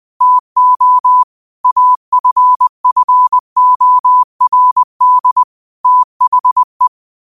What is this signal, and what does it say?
Morse code, heard as a keyed tone, about 20 wpm, 1000 Hz. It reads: TO AFFORD THE
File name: qotdfiles / 20wpm